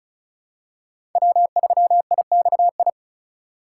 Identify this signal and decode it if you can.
Morse code, heard as a keyed tone, about 36 wpm, 700 Hz. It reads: W3IXI
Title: USA Call Signs - 5 Characters - 3x Character Spacing - 35wpm